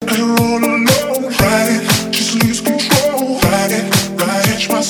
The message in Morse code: none